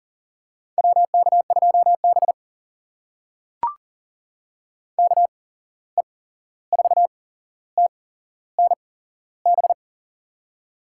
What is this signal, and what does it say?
Morse code, heard as a keyed tone, about 40 wpm, spaced as for 15 wpm, 700 Hz. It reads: WK2B KE4TNB
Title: USA Call Signs - 8x Character Spacing - 40wpm